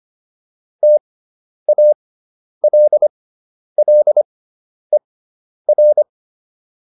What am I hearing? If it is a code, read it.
Morse code, heard as a keyed tone, about 25 wpm, spaced as for 10 wpm, 600 Hz. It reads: TALLER